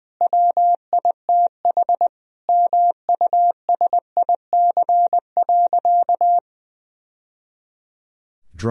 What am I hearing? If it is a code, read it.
Morse code, heard as a keyed tone, about 20 wpm, 700 Hz. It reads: WITH MUSIC.
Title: Sentences from Top 1000 Words - Review Part 1 - 20wpm